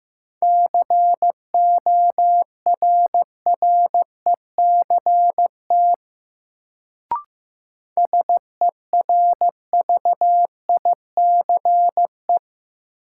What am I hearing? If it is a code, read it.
Morse code, heard as a keyed tone, about 15 wpm, 700 Hz. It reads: CORRECT SERVICE
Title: Sentences from Top 700 Words - New Word Drill - 15wpm